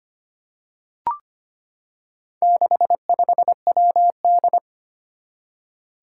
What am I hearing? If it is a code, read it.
Morse code, heard as a keyed tone, about 25 wpm, 700 Hz. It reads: 65WD
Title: Four Challenging Letter-Number Combinations 25wpm